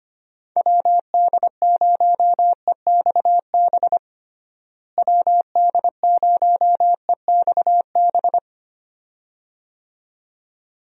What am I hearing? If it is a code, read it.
Morse code, heard as a keyed tone, about 25 wpm, 700 Hz. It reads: WD0EXB WD0EXB
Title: USA Call Signs - 6 Characters - Repeated 2x - 25wpm